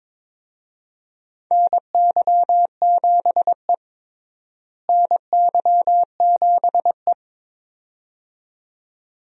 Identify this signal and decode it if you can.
Morse code, heard as a keyed tone, about 22 wpm, 700 Hz. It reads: NY7E NY7E
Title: USA Call Signs - 4 Characters - Repeated 2x - 22wpm